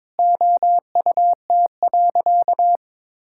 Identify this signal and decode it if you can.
Morse code, heard as a keyed tone, about 22 wpm, 700 Hz. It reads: OUT.
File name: Sentences with 7-Word Phrases - 22wpm